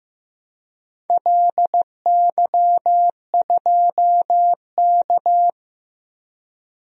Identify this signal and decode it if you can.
Morse code, heard as a keyed tone, about 16 wpm, 700 Hz. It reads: LY2K
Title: DXCC-100 Call Signs - 4 Characters - Repeated 3x - 15wpm